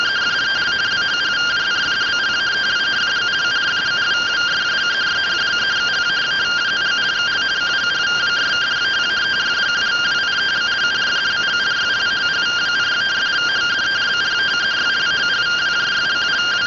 Two-tone teleprinter signal, about 170 Hz shift.